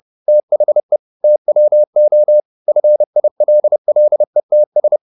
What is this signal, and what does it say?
Morse code, heard as a keyed tone, about 30 wpm, 600 Hz. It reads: THE TWO FILLETS